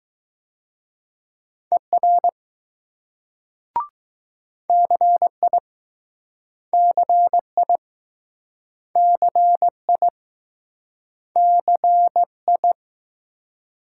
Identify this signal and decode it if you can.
Morse code, heard as a keyed tone, about 20 wpm, 700 Hz. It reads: ER CI CI CI CI